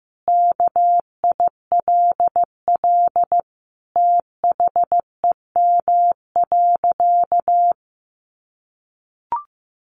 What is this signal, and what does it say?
Morse code, heard as a keyed tone, about 15 wpm, 700 Hz. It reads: KILL THEM.
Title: Easy Phrases 15wpm